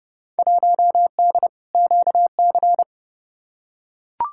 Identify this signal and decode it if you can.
Morse code, heard as a keyed tone, about 30 wpm, 700 Hz. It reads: 1D QC
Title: Field Day Exchanges - 30wpm